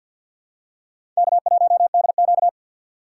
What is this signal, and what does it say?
Morse code, heard as a keyed tone, about 52 wpm, 690 Hz. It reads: K1DQ
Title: USA Call Signs - 4 Characters - 6x Character Spacing - 50wpm